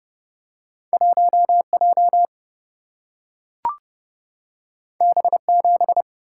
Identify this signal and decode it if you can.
Morse code, heard as a keyed tone, about 30 wpm, 700 Hz. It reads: 1 J B 7
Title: Single-Letter Single-Number 30wpm